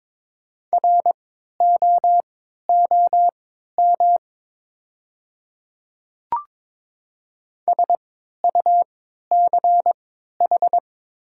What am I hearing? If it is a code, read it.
Morse code, heard as a keyed tone, about 22 wpm, spaced as for 13 wpm, 700 Hz. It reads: ROOM SUCH
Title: 3x Character Spacing - Top 300 Words - 22wpm